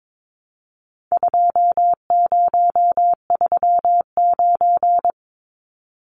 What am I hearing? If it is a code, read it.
Morse code, heard as a keyed tone, about 22 wpm, 700 Hz. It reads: 2039
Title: Numbers - 4-Digit 22wpm